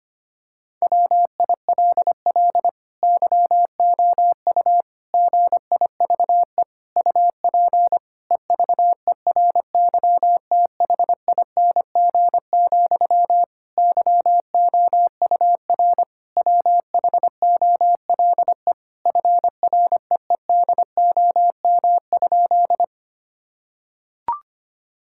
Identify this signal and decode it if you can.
Morse code, heard as a keyed tone, about 25 wpm, 700 Hz. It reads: WILL YOU GIVE UP EVERYTHING, YOUR WHOLE FREEDOM?
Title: Sentences from Top 1000 Words - Review Part 5 - 25wpm